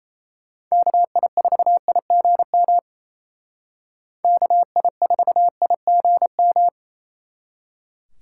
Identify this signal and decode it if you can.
Morse code, heard as a keyed tone, about 36 wpm, 700 Hz. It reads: KI4IGM KI4IGM